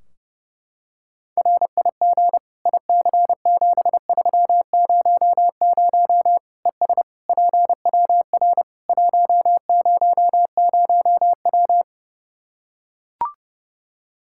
Morse code, heard as a keyed tone, about 30 wpm, 700 Hz. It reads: RIG IC7300 ES PWR 100W
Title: Practice QSO - Exchange 2 - Info - 30wpm